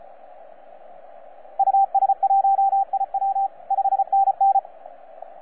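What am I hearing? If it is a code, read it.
Morse code, heard as a keyed tone, about 34 wpm, 780 Hz, with radio noise. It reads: US1IW 5NN
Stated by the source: FS/LJ2IJ heard on 12 metres CW, from DX World.net.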